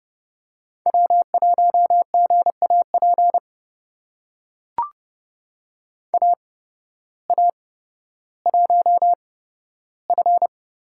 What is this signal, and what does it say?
Morse code, heard as a keyed tone, about 30 wpm, spaced as for 14 wpm, 700 Hz. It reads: W1GAP AA1F
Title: USA Call Signs - 5 Characters - 8x Character Spacing - 30wpm